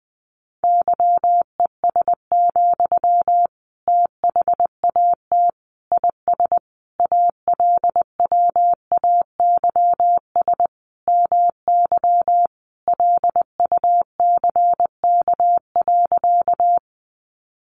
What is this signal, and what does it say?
Morse code, heard as a keyed tone, about 20 wpm, 700 Hz. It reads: YES, THAT IS ALWAYS MY LUCK.
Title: Sentences from Top 1500 Words 20wpm